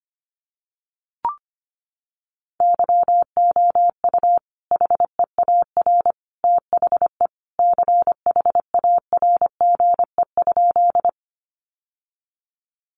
Morse code, heard as a keyed tone, about 25 wpm, 700 Hz. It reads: YOU HEAR THE CHARGE?
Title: Sentences from Top 900 Words 25wpm